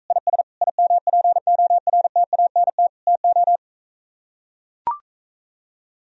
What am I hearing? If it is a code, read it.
Morse code, heard as a keyed tone, about 45 wpm, 700 Hz. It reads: IS IMPORTANT TO